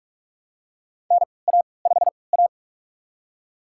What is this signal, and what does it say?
Morse code, heard as a keyed tone, about 44 wpm, spaced as for 25 wpm, 700 Hz. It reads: NA5A
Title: USA Call Signs - 4 Characters - 3x Character Spacing - 45wpm